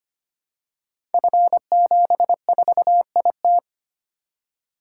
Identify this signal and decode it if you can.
Morse code, heard as a keyed tone, about 25 wpm, 700 Hz. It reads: F74IT
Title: Code Groups - 5 Letter:Numbers - 25wpm